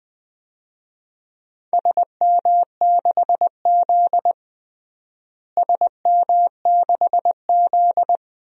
Morse code, heard as a keyed tone, about 20 wpm, 700 Hz. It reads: SM6Z SM6Z
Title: DXCC-100 Call Signs - 4 Characters - Repeated 2x - 20wpm